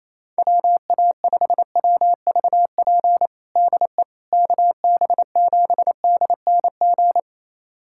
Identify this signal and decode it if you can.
Morse code, heard as a keyed tone, about 28 wpm, 700 Hz. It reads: WA5WVP DE KB7DNG
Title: USA Callsign DE Callsign - 6 Characters - 28wpm